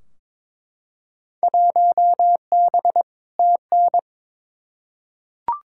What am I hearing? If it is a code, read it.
Morse code, heard as a keyed tone, about 22 wpm, 700 Hz. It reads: 1B TN
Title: Field Day Exchanges - 22wpm